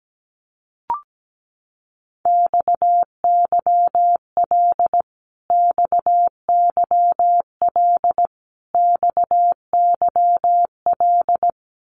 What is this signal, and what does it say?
Morse code, heard as a keyed tone, about 17 wpm, 700 Hz. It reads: XYL XYL XYL
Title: Abbreviations Repeated 3x - 17wpm